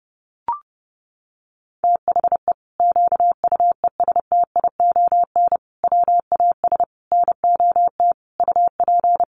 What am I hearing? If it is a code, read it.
Morse code, heard as a keyed tone, about 30 wpm, 700 Hz. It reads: THE QUESTION WAS NOT UP
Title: Sentences from Top 1200 Words 30wpm